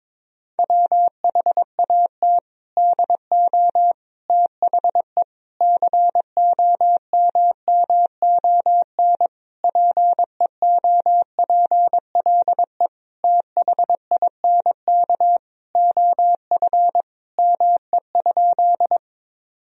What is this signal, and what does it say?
Morse code, heard as a keyed tone, about 22 wpm, 700 Hz. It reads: WHAT DO THE COMMON PEOPLE THINK OF ME?
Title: Sentences from Top 1000 Words - Review Part 3 - 22wpm